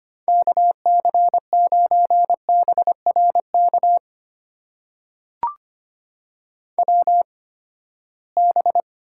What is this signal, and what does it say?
Morse code, heard as a keyed tone, about 25 wpm, spaced as for 17 wpm, 700 Hz. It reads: KC9BRK WB
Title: USA Call Signs - 6 Characters - 8x Character Spacing - 25wpm